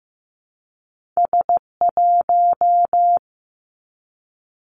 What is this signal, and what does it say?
Morse code, heard as a keyed tone, about 15 wpm, 700 Hz. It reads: S1
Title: Two Challenging Letter-Number Combinations 15wpm